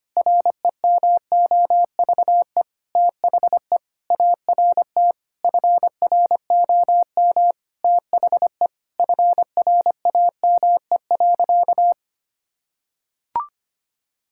Morse code, heard as a keyed tone, about 25 wpm, 700 Hz. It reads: REMOVE THE ART FROM THE FRAME.
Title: Sentences from Top 1000 Words - Review Part 6 - 25wpm